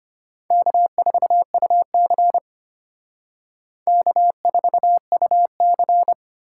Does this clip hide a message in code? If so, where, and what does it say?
Yes: Morse code from 0.5 to 6.1 s, heard as a keyed tone, about 32 wpm, 700 Hz. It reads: K4UC K4UC